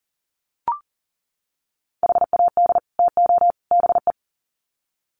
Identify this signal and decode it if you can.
Morse code, heard as a keyed tone, about 40 wpm, 700 Hz. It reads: HAD TO BE
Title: Sets of 3 Words 40wpm